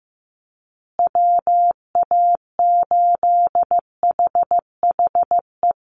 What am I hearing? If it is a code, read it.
Morse code, heard as a keyed tone, about 15 wpm, 700 Hz. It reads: WA8HHE
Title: USA Call Signs Repeated 2x 15wpm